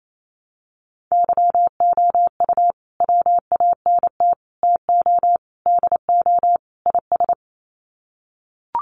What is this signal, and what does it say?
Morse code, heard as a keyed tone, about 28 wpm, 700 Hz. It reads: YOU WANT TO DO IS
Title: Sets of 5 Words Encore2 28wpm